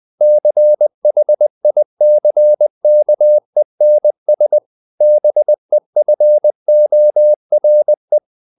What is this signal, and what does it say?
Morse code, heard as a keyed tone, about 20 wpm, 600 Hz. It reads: CHICKENS BEFORE
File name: CW / Practice / EnglishSayings20